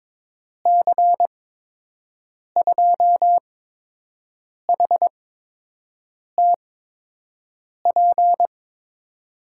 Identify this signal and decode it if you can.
Morse code, heard as a keyed tone, about 22 wpm, spaced as for 6 wpm, 700 Hz. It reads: C2HTP
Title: USA Call Signs - 6 Characters - 8x Character Spacing - 22wpm